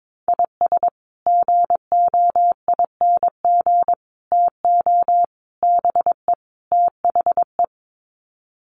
Morse code, heard as a keyed tone, about 22 wpm, 700 Hz. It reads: IS GOING TO BE THE
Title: Sets of 5 Words 22wpm